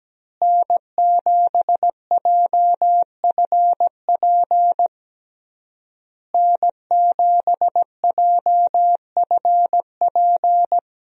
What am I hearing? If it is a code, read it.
Morse code, heard as a keyed tone, about 17 wpm, 700 Hz. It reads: N7JFP N7JFP
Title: USA Call Signs - 5 Characters - Repeated 3x - 17wpm